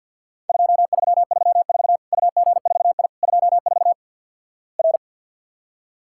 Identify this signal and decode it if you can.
Morse code, heard as a keyed tone, about 47 wpm, 700 Hz. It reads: 2334 UG4I 24 R
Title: Worked All Europe DX Contest - QTC Exchange with Repeats - 50wpm